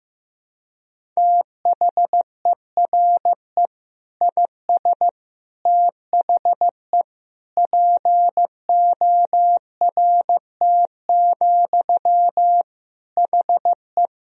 Morse code, heard as a keyed tone, about 15 wpm, 700 Hz. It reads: THERE IS THE PORT, HE
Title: Sentences from Top 1000 Words - Review Part 1 - 15wpm